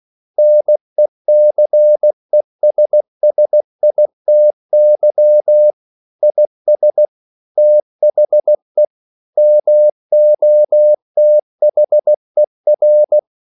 Morse code, heard as a keyed tone, about 16 wpm, 600 Hz. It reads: NECESSITY IS THE MOTHER
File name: CW/Practice